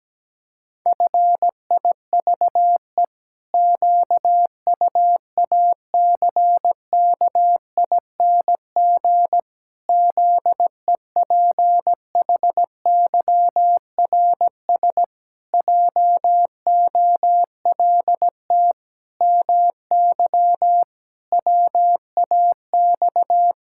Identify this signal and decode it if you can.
Morse code, heard as a keyed tone, about 17 wpm, 700 Hz. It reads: FIVE QUACKING ZEPHYRS JOLT MY WAX
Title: Pangram Challenge v2 - 17wpm